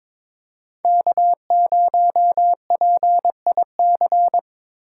Morse code, heard as a keyed tone, about 25 wpm, 700 Hz. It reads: K0PIC
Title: Speed Racing - USA Call Signs 22wpm